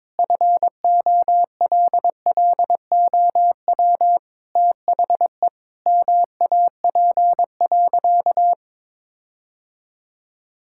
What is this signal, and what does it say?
Morse code, heard as a keyed tone, about 22 wpm, 700 Hz. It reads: FOLLOW THE MAP.
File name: practice / 22wpm / Sentences with 8-Word Phrases - 22wpm